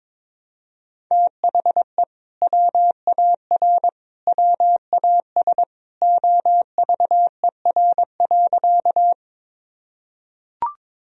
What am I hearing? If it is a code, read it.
Morse code, heard as a keyed tone, about 22 wpm, 700 Hz. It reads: THE WAR WAS OVER.
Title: Sentences from Top 1000 Words - Review Part 2 - 22wpm